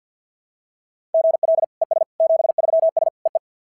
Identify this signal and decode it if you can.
Morse code, heard as a keyed tone, about 53 wpm, 690 Hz. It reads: GL ES 73S EE
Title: SOTA QSO Practice 50wpm